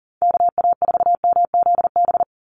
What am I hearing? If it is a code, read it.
Morse code, heard as a keyed tone, about 40 wpm, 700 Hz. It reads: KA4MZB
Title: USA Call Signs 40wpm